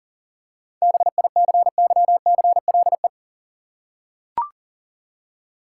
Morse code, heard as a keyed tone, about 40 wpm, 700 Hz. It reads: BICYCLE
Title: Seven Letter Words - 40wpm